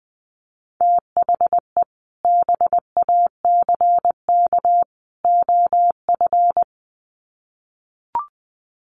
Morse code, heard as a keyed tone, about 20 wpm, 700 Hz. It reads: THE BACK OF
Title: Sets of 3 Words 20wpm